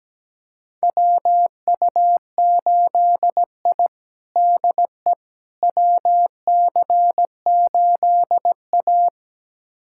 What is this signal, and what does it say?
Morse code, heard as a keyed tone, about 17 wpm, 700 Hz. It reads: WU8I DE WC8A
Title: USA Callsign DE Callsign - 4 Characters - 17wpm